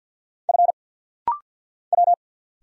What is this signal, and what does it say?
Morse code, heard as a keyed tone, about 50 wpm, 700 Hz. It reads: F W
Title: Single Letters - Rapid-Fire 50wpm